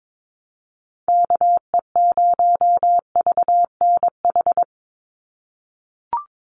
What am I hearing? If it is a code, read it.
Morse code, heard as a keyed tone, about 22 wpm, 700 Hz. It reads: KE0VNH
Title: USA Call Signs 22wpm